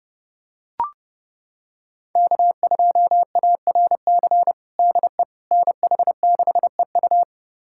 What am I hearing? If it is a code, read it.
Morse code, heard as a keyed tone, about 30 wpm, 700 Hz. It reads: K2ARC DE NH6EU
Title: USA Callsign DE Callsign - 5 Characters - 30wpm